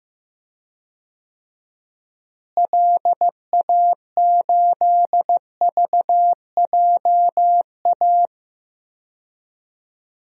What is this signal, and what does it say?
Morse code, heard as a keyed tone, about 15 wpm, 700 Hz. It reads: LA8VJA
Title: DXCC-100 Call Signs - 6 Characters - Repeated 3x - 15wpm